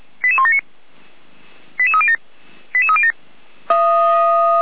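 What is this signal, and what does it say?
Five-tone selective call (ZVEI), heard as quick runs of five tones, tones of about 80 ms. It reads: 88179 88307 88307
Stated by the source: ZVEI five-tone selective call received on SDR.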